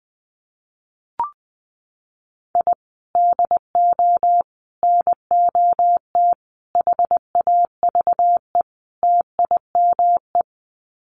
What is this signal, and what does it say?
Morse code, heard as a keyed tone, about 20 wpm, 700 Hz. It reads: I DO NOT HAVE TIME
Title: Sets of 5 Words Encore 20wpm